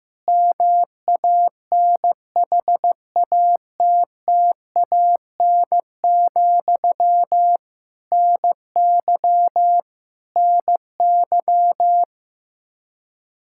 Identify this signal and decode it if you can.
Morse code, heard as a keyed tone, about 15 wpm, 700 Hz. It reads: MANHATTAN, NY NY
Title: Practice QSO - Exchange 1 - Beginning - 15wpm